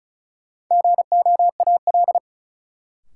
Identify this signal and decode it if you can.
Morse code, heard as a keyed tone, about 36 wpm, 700 Hz. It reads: GOAL
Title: Sentences from Top 1000 Words - Review Part 5 - New Word Drill - 35wpm